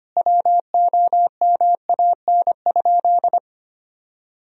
Morse code, heard as a keyed tone, about 25 wpm, 700 Hz. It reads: WOMAN?
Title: Sentences from Top 1000 Words - Review Part 3 - 25wpm